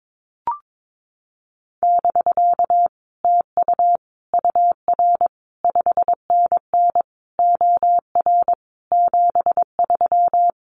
Morse code, heard as a keyed tone, about 22 wpm, 700 Hz. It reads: <BK> TU UR 5NN OR 73
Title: Common QSO Phrases Encore 22wpm